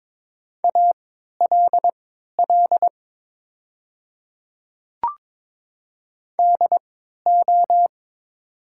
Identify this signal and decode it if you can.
Morse code, heard as a keyed tone, about 22 wpm, spaced as for 12 wpm, 700 Hz. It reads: ALL DO
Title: 3x Character Spacing - Top 100 Words - 22wpm